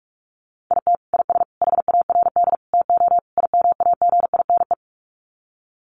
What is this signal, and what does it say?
Morse code, heard as a keyed tone, about 44 wpm, 700 Hz. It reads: IT IS HARD TO IMAGINE
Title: Sets of 5 Words Encore2 45wpm